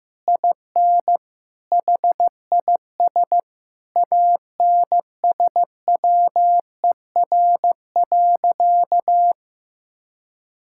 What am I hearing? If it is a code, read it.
Morse code, heard as a keyed tone, about 15 wpm, 700 Hz. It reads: IN HIS ANSWER.
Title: Sentences from Top 1600 Words 15wpm